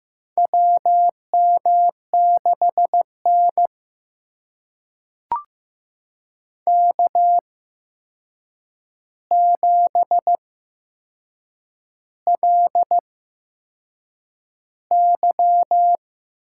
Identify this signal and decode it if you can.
Morse code, heard as a keyed tone, about 15 wpm, spaced as for 6 wpm, 700 Hz. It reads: WM6N K7LY
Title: USA Call Signs - 4 Characters - 8x Character Spacing - 15wpm